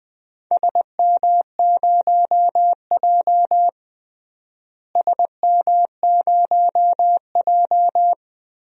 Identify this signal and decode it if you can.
Morse code, heard as a keyed tone, about 20 wpm, 700 Hz. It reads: SM0J SM0J
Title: DXCC-100 Call Signs - 4 Characters - Repeated 3x - 20wpm